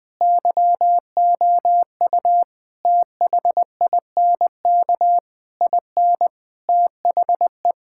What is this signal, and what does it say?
Morse code, heard as a keyed tone, about 20 wpm, 700 Hz. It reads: YOU THINK IN THE
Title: Sets of 9 Words - 20wpm